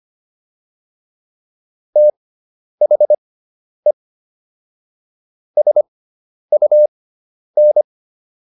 Morse code, heard as a keyed tone, about 25 wpm, spaced as for 10 wpm, 600 Hz. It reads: THE SUN
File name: QSO 410 10